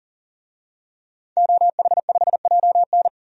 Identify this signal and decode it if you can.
Morse code, heard as a keyed tone, about 40 wpm, 700 Hz. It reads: OH5JN
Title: DXCC-100 Call Signs - 5 Characters - 40wpm